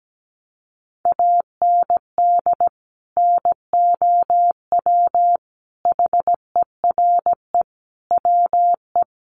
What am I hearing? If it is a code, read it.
Morse code, heard as a keyed tone, about 17 wpm, 700 Hz. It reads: AND NOW HERE WE
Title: Sentences from Top 1400 Words 17wpm